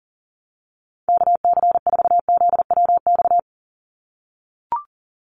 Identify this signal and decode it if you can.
Morse code, heard as a keyed tone, about 40 wpm, 700 Hz. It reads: KC4ZWX
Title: USA Call Signs 40wpm